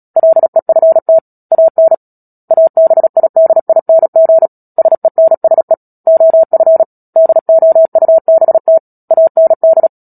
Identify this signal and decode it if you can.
Morse code, heard as a keyed tone, about 37 wpm, 650 Hz. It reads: LEFT AN ABIDING SENSE OF DOUBT AND
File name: Book Two / 10-Chapter 10 - Epilogue - 36 WPM